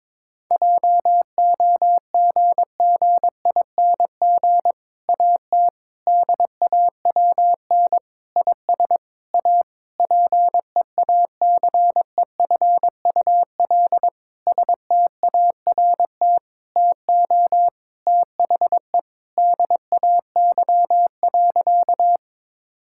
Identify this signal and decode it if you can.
Morse code, heard as a keyed tone, about 22 wpm, 700 Hz. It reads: JOGGING AT DAWN IS A PEACEFUL START TO THE DAY.
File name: Difficult Sentences with Hints - 22wpm